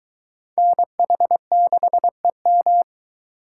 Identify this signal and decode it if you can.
Morse code, heard as a keyed tone, about 22 wpm, 700 Hz. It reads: NH6EM